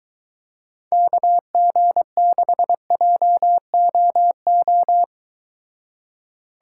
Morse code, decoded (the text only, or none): KG6JOO